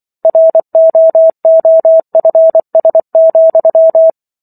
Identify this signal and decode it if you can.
Morse code, heard as a keyed tone, about 24 wpm, 650 Hz. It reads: ROOFS,